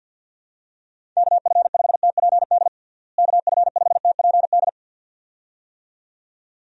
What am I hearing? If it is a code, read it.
Morse code, heard as a keyed tone, about 52 wpm, 700 Hz. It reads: KF5TPD KF5TPD
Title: USA Call Signs - 6 Characters - Repeated 2x - 50wpm